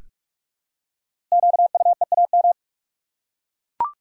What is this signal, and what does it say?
Morse code, heard as a keyed tone, about 44 wpm, 700 Hz. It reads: QUEAM
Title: Code Groups - 5 Letters - 45wpm